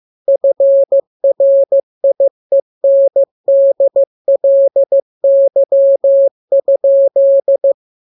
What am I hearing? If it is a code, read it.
Morse code, heard as a keyed tone, about 15 wpm, 550 Hz. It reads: FRIENDLY?
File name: jokes / joke 23 15 0000